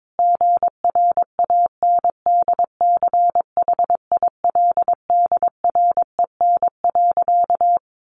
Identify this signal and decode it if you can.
Morse code, heard as a keyed tone, about 22 wpm, 700 Hz. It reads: GRANDCHILDREN.
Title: A Sound Of Thunder 22wpm